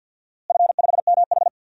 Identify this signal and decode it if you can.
Morse code, heard as a keyed tone, about 53 wpm, 700 Hz. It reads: F5MH
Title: DXCC-100 Call Signs - 4 Characters - Repeated 3x - 50wpm